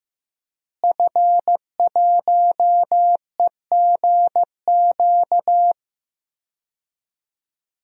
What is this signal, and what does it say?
Morse code, heard as a keyed tone, about 15 wpm, 700 Hz. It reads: F1EGQ
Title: DXCC-100 Call Signs - 5 Characters - Repeated 2x - 15wpm